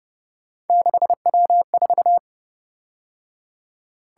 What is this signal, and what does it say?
Morse code, heard as a keyed tone, about 29 wpm, 700 Hz. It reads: 6W4
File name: Speed-Racing - Three Letter-Number Combinations - 25wpm